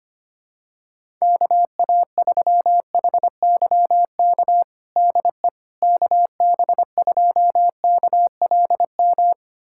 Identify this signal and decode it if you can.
Morse code, heard as a keyed tone, about 25 wpm, 700 Hz. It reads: KA3HYK DE KB2KLM
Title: USA Callsign DE Callsign - 6 Characters - 25wpm